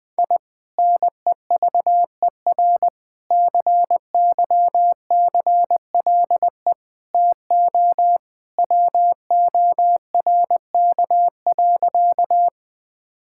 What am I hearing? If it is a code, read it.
Morse code, heard as a keyed tone, about 20 wpm, 700 Hz. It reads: I NEVER CYCLE TO WORK.
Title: Sentences from Top 1000 Words - Review Part 6 - 20wpm